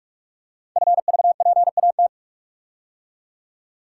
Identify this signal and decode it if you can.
Morse code, heard as a keyed tone, about 44 wpm, 700 Hz. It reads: FVPAT
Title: Code Groups - 5 Letters - 45wpm